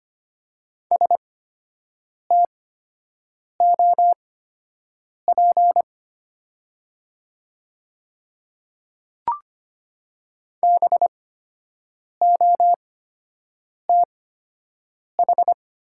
Morse code, heard as a keyed tone, about 25 wpm, spaced as for 7 wpm, 700 Hz. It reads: STOP BOTH
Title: 8x Character Spacing - Top 300 Words - 25wpm